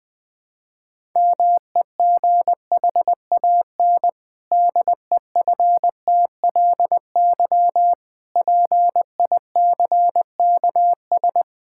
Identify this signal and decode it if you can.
Morse code, heard as a keyed tone, about 20 wpm, 700 Hz. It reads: MEGHAN DEFTLY PICKS
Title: Pangram Challenge v2 - 20wpm